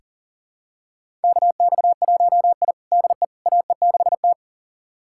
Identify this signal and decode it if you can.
Morse code, heard as a keyed tone, about 40 wpm, 700 Hz. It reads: KX1I DE AE6T
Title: USA Callsign DE Callsign - 4 Characters - 40wpm